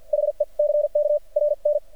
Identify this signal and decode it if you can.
Morse code, heard as a keyed tone, about 48 wpm, spaced as for 40 wpm, 590 Hz. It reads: HECK RN
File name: archived / CRSnet